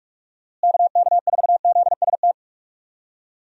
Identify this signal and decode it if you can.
Morse code, heard as a keyed tone, about 44 wpm, 700 Hz. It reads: KK4ZST
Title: USA Call Signs - 6 Characters - Repeated 3x - 45wpm